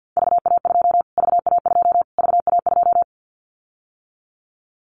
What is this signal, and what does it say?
Morse code, heard as a keyed tone, about 50 wpm, 700 Hz. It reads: VA2 VA2 VA2
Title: Callsign Prefixes for Top 25 DXCC - Repeated 3x - 50wpm